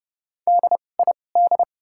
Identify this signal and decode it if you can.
Morse code, heard as a keyed tone, about 30 wpm, spaced as for 22 wpm, 700 Hz. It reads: DID
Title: 2x Character Spacing - Top 100 Words - 30wpm